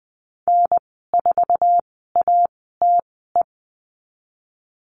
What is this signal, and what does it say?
Morse code, heard as a keyed tone, about 19 wpm, spaced as for 14 wpm, 700 Hz. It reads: N4ATE
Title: 167 Lesson 27g - Number 7 - Call Signs 20wpm